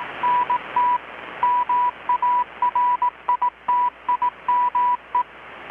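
Morse code, heard as a keyed tone, about 18 wpm, 980 Hz, with radio noise. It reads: NT MARITIME
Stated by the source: Morse from KPH's Night of Nights XX maritime radiotelegraphy event, received in Chiapas, Mexico about 3,800 km away on a Kenwood R-600 with a grounded 10 m long-wire antenna.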